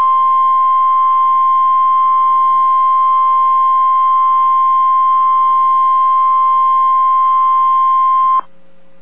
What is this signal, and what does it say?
Emergency Alert System (SAME) tones (the 1050 Hz attention tone). It is heard from 0.6 to 8.3 s.